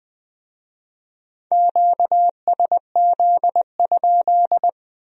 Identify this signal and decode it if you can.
Morse code, heard as a keyed tone, about 21 wpm, 700 Hz. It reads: QSZ?
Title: Speed-Racing Q-Codes - 17wpm